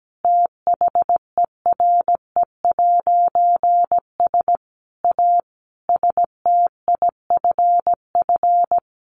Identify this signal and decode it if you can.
Morse code, heard as a keyed tone, about 17 wpm, 700 Hz. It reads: THERE'S A STIFF
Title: A Sound Of Thunder 17wpm